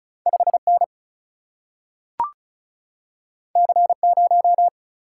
Morse code, heard as a keyed tone, about 36 wpm, 700 Hz. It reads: 5 N C 0
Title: Single-Letter Single-Number 35wpm